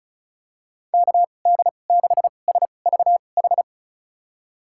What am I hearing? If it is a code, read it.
Morse code, heard as a keyed tone, about 36 wpm, spaced as for 25 wpm, 700 Hz. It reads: KD6SVH
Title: USA Call Signs - 6 Characters - 2x Character Spacing - 35wpm